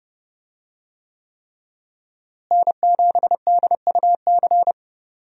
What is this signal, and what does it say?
Morse code, heard as a keyed tone, about 30 wpm, 700 Hz. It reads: N7DUC
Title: USA Call Signs - 5 Characters - Repeated 3x - 30wpm